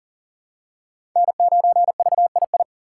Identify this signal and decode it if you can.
Morse code, heard as a keyed tone, about 40 wpm, 700 Hz. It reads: N9VII
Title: USA Call Signs - 5 Characters - 6x Character Spacing - 40wpm